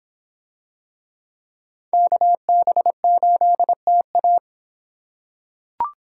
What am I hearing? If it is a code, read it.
Morse code, heard as a keyed tone, about 26 wpm, 700 Hz. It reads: KB8TA